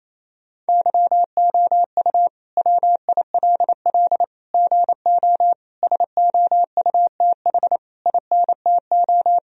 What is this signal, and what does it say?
Morse code, heard as a keyed tone, about 28 wpm, 700 Hz. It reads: YOU WILL GO SOUTH INTO
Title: Sentences from Top 1000 Words - Review Part 2 - 28wpm